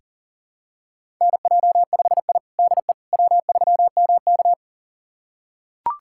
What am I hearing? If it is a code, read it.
Morse code, heard as a keyed tone, about 40 wpm, 700 Hz. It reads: NJ5I DE W3MK
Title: USA Callsign DE Callsign - 4 Characters - 40wpm